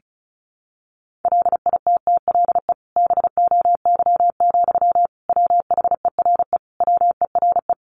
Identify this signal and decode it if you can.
Morse code, heard as a keyed tone, about 34 wpm, 700 Hz. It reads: LITTLE BOY, WHERE WERE
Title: Sentences from Top 1000 Words 35wpm